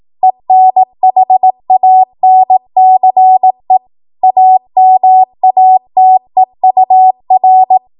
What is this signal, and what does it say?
Morse code, heard as a keyed tone, about 18 wpm, 750 Hz. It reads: ENHANCE AMATEUR
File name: Morse / 18WPM